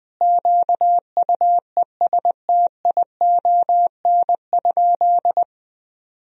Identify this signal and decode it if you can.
Morse code, heard as a keyed tone, about 20 wpm, 700 Hz. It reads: QUESTION?
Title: Sentences from Top 400 Words and Abbreviations 20wpm